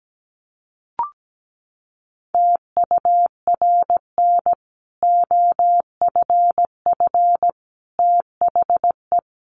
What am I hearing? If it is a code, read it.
Morse code, heard as a keyed tone, about 17 wpm, 700 Hz. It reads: TURN OFF THE
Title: Sentences from Top 1100 Words 17wpm